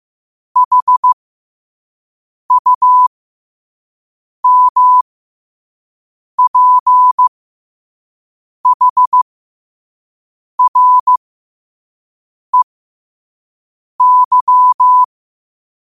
Morse code, heard as a keyed tone, about 15 wpm, spaced as for 5 wpm, 1000 Hz. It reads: HUMPHREY
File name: qotdfiles / 5wpm